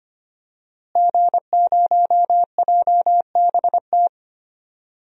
Morse code, heard as a keyed tone, about 25 wpm, 700 Hz. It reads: G0JBT
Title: DXCC-100 Call Signs - 5 Characters - Repeated 3x - 25wpm